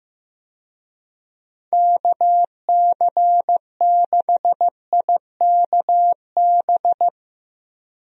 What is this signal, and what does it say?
Morse code, heard as a keyed tone, about 15 wpm, 700 Hz. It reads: KC6IKB
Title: USA Call Signs - 6 Characters - Repeated 3x - 15wpm